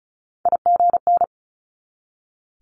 Morse code, heard as a keyed tone, about 36 wpm, 700 Hz. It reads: IGN
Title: Multi-letter Phonemes 35wpm